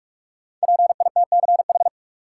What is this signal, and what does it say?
Morse code, heard as a keyed tone, about 45 wpm, 700 Hz. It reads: PITCH
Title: Sentences from Top 1300 Words - New Word Drill Alt - 45wpm